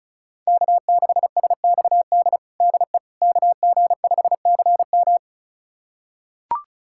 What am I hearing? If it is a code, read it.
Morse code, heard as a keyed tone, about 36 wpm, 700 Hz. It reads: K6SXD DE KG5CM
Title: USA Callsign DE Callsign - 5 Characters - 35wpm